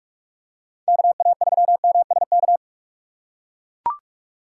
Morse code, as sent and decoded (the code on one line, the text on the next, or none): -.- .- ...-- -- ... -.-
KA3MSK